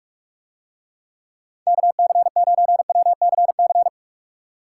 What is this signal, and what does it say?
Morse code, heard as a keyed tone, about 47 wpm, 700 Hz. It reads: KC9WCC